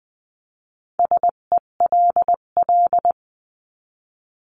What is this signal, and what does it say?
Morse code, heard as a keyed tone, about 20 wpm, 700 Hz. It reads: SELL
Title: 258 Lesson 42c - Words 16-20wpm